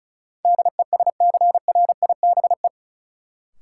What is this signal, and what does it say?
Morse code, heard as a keyed tone, about 34 wpm, 700 Hz. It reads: DESCRIBE
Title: Sentences from Top 1000 Words - New Word Drill - 35wpm